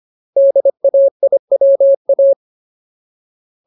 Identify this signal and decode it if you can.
Morse code, heard as a keyed tone, about 25 wpm, 550 Hz. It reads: DAIWA
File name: Ultimate Word List / 25WPM Ultimate